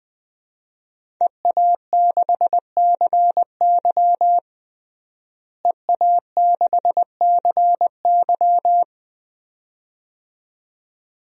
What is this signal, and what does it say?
Morse code, heard as a keyed tone, about 20 wpm, 700 Hz. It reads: EA6CY EA6CY
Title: DXCC-100 Call Signs - 5 Characters - Repeated 2x - 20wpm